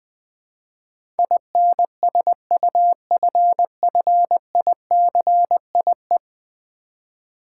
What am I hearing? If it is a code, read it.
Morse code, heard as a keyed tone, about 20 wpm, 700 Hz. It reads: INSUFFICIE
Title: Build A Word - 5 Letters At A Time - 12-Letter Words - 20wpm